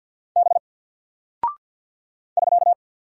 Morse code, heard as a keyed tone, about 48 wpm, 700 Hz. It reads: B <SK>
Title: Single Character (Letter:Number:Punctuation:Prosign) Alt 50wpm